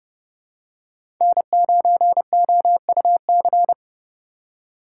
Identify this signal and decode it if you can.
Morse code, heard as a keyed tone, about 30 wpm, 700 Hz. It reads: N9OUC
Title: USA Call Signs - 3x Character Spacing - 30wpm